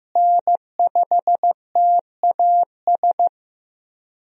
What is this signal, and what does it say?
Morse code, heard as a keyed tone, about 15 wpm, 700 Hz. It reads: N5TAS
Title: USA Call Signs - 5 Characters - Repeated 3x - 15wpm